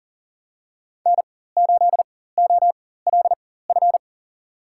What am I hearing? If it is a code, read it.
Morse code, heard as a keyed tone, about 40 wpm, spaced as for 19 wpm, 700 Hz. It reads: N8OLF
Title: USA Call Signs - 5 Characters - 4x Character Spacing - 40wpm